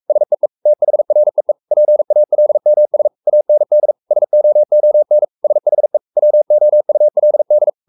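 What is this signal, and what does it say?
Morse code, heard as a keyed tone, about 44 wpm, 600 Hz. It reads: SEE THREE PALMS AND SOON SHE WOULD
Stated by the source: Morse code at 43 wpm sending Ken Follett's novel Never in English, from CW med Gnister.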